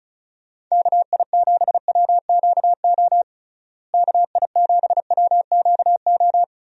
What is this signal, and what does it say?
Morse code, heard as a keyed tone, about 36 wpm, 700 Hz. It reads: KI7WQO KI7WQO
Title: USA Call Signs - 6 Characters - Repeated 3x - 35wpm